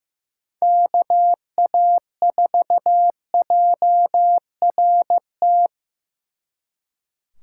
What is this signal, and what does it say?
Morse code, heard as a keyed tone, about 18 wpm, 700 Hz. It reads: KA4JRT
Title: Speed Racing - USA 2x3 Call Signs 15wpm